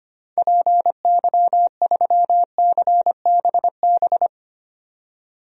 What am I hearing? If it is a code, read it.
Morse code, heard as a keyed tone, about 25 wpm, 700 Hz. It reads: PY3CBB
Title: DXCC-100 Call Signs - 6 Characters - 25wpm